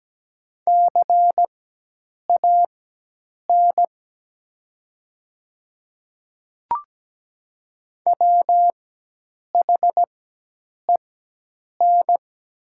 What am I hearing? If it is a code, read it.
Morse code, heard as a keyed tone, about 17 wpm, spaced as for 8 wpm, 700 Hz. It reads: CAN WHEN
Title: 4x Character Spacing - Top 300 Words - 17wpm